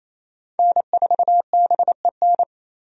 Morse code, heard as a keyed tone, about 28 wpm, 700 Hz. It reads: N4BEN
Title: USA Call Signs - 5 Characters - 28wpm